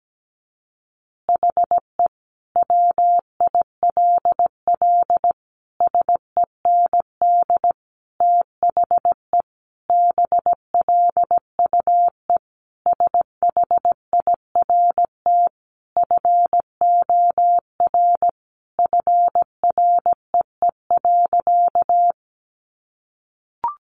Morse code, heard as a keyed tone, about 17 wpm, 700 Hz. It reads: HE WILL SEND THE BLUE SHIRT FOR FREE.
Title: Additional Easy Phrases 17wpm